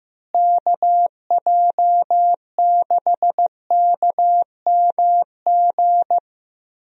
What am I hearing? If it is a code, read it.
Morse code, heard as a keyed tone, about 15 wpm, 700 Hz. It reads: KJ6KMG
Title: USA Call Signs - 6 Characters - Repeated 2x - 15wpm